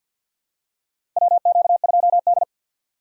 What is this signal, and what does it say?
Morse code, heard as a keyed tone, about 51 wpm, 700 Hz. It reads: WQ2D
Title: USA Call Signs - 2x Character Spacing - 50wpm